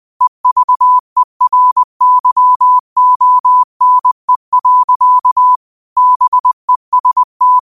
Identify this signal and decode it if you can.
Morse code, heard as a keyed tone, about 20 wpm, 1000 Hz. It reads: EVERYONE. BEST
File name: qotdfiles / 20wpm